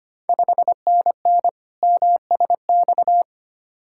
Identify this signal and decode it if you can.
Morse code, heard as a keyed tone, about 25 wpm, 690 Hz. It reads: 5NN MSX
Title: Ontario QSO Party Exchange 25wpm